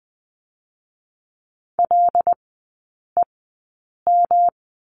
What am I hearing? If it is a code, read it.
Morse code, heard as a keyed tone, about 20 wpm, 700 Hz. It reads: L E M
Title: 114 Lesson 19e - Letter w - Three Characters 20wpm